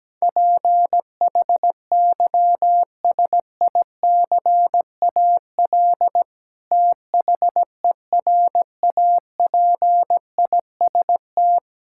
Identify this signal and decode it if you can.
Morse code, heard as a keyed tone, about 17 wpm, 700 Hz. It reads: PHYSICAL THERAPIST
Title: K1USN SST QSO Practice 17wpm